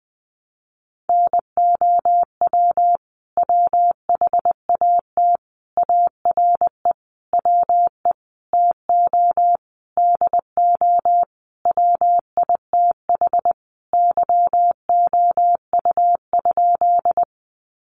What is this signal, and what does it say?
Morse code, heard as a keyed tone, about 20 wpm, 700 Hz. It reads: NOW WHAT ARE WE TO DO WITH YOU?
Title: Sentences from Top 100 Words 20wpm